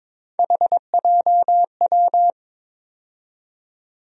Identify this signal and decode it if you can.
Morse code, heard as a keyed tone, about 22 wpm, 700 Hz. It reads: HJW
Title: USA Call Sign Postfixes - 3 Letters 22wpm